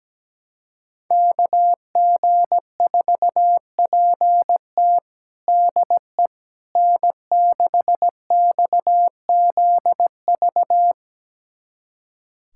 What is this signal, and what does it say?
Morse code, heard as a keyed tone, about 17 wpm, 700 Hz. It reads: KG4PT DE N6XZV
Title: USA Callsign DE Callsign - 5 Characters - 17wpm